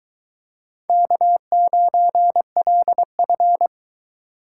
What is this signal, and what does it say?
Morse code, heard as a keyed tone, about 21 wpm, 700 Hz. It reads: K9LF